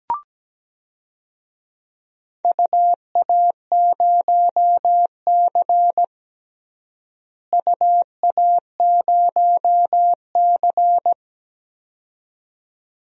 Morse code, heard as a keyed tone, about 17 wpm, 700 Hz. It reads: UA0C UA0C
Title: DXCC-100 Call Signs - 4 Characters - Repeated 2x - 17wpm